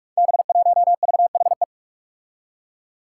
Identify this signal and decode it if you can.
Morse code, heard as a keyed tone, about 44 wpm, 700 Hz. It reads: B1VHE
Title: Five Challenging Letter-Number Combinations 45wpm